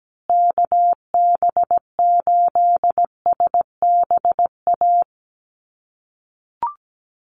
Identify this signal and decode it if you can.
Morse code, heard as a keyed tone, about 17 wpm, 700 Hz. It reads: KB8SBA
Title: USA Call Signs Repeated 3x 17wpm